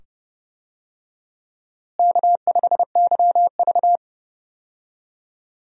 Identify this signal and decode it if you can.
Morse code, heard as a keyed tone, about 30 wpm, 700 Hz. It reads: K5YV
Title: USA Call Signs - 4 Characters - Repeated 2x - 30wpm